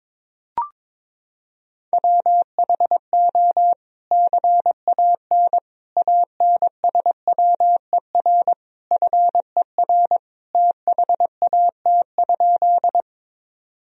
Morse code, heard as a keyed tone, about 22 wpm, 700 Hz. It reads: WHO CAN ANSWER FER THAT?
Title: Sentences from Top 300 Words and Abbreviations 22wpm